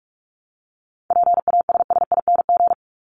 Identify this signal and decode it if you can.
Morse code, heard as a keyed tone, about 44 wpm, 700 Hz. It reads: PASSING
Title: ING Suffix - Top 100 Words with Base Words 45wpm